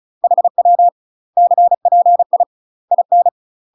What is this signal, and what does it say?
Morse code, heard as a keyed tone, about 37 wpm, 700 Hz. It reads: HW CPI IN
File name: UK MASTER QSOs TRAINING / 35WPM / UK MASTER QSOs 0015